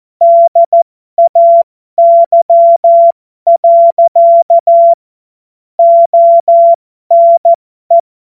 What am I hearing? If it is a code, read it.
Morse code, heard as a keyed tone, about 14 wpm, spaced as for 12 wpm, 680 Hz. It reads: DAY. ONE
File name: Morsekurs / Witze